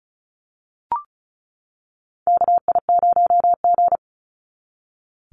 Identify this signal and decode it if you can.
Morse code, heard as a keyed tone, about 36 wpm, 700 Hz. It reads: KI0G
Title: NA SOTA Chaser Callsigns 35wpm